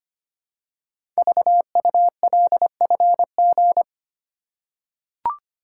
Morse code, heard as a keyed tone, about 25 wpm, 700 Hz. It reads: VULFG
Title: Code Groups - 5 Letters - 25wpm